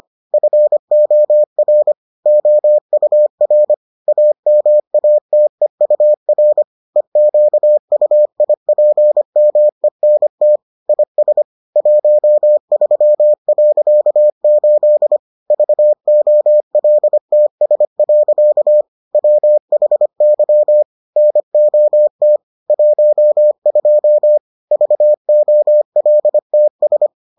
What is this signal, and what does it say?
Morse code, heard as a keyed tone, about 25 wpm, 600 Hz. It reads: FOR OUR AMATEUR EQUIPMENT IS 13.8 VOLTS. WHY NOT 12 VOLTS